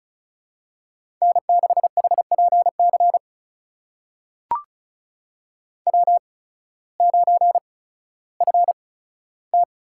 Morse code, heard as a keyed tone, about 36 wpm, spaced as for 16 wpm, 700 Hz. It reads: N6HPC W9FT
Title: USA Call Signs - 5 Characters - 8x Character Spacing - 35wpm